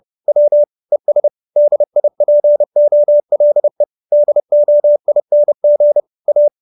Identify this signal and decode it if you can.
Morse code, heard as a keyed tone, about 30 wpm, 590 Hz. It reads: W ES DIPOLE DOING A
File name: QSO308 30